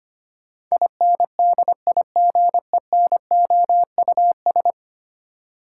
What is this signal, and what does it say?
Morse code, heard as a keyed tone, about 25 wpm, 700 Hz. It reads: INDIGENOUS
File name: practice / 25wpm / Ten Letter Words - 25wpm